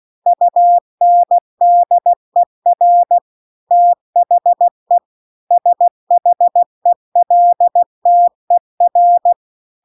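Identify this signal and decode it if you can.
Morse code, heard as a keyed tone, about 16 wpm, 700 Hz. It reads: UNDER THE SHELTER
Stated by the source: Morse practice text at 16 WPM.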